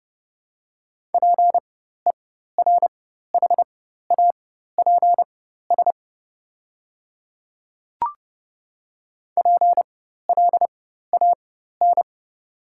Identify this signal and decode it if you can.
Morse code, heard as a keyed tone, about 30 wpm, spaced as for 14 wpm, 700 Hz. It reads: PERHAPS PLAN
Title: 4x Character Spacing - Top 500 Words - 30wpm